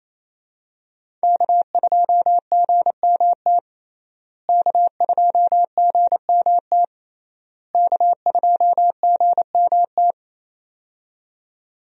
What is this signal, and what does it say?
Morse code, heard as a keyed tone, about 27 wpm, 700 Hz. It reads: K2GMT K2GMT K2GMT
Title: USA Call Signs - 5 Characters - Repeated 3x - 28wpm